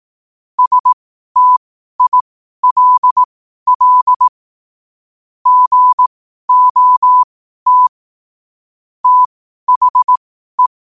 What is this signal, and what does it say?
Morse code, heard as a keyed tone, about 18 wpm, spaced as for 12 wpm, 1000 Hz. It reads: STILL GOT THE
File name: qotdfiles / 12wpm